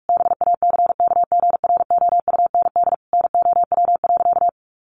Morse code, heard as a keyed tone, about 44 wpm, 700 Hz. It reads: BACKGROUND NOW.
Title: Sentences from Top 1300 Words 45wpm